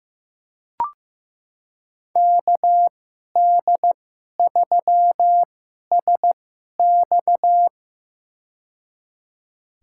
Morse code, heard as a keyed tone, about 15 wpm, spaced as for 11 wpm, 700 Hz. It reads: KD3SX
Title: USA Call Signs - 5 Characters - 2x Character Spacing - 15wpm